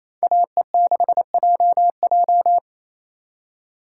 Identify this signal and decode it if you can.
Morse code, heard as a keyed tone, about 34 wpm, 700 Hz. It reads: AE6JJ
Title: Speed Racing - USA 2x2 Call Signs 28wpm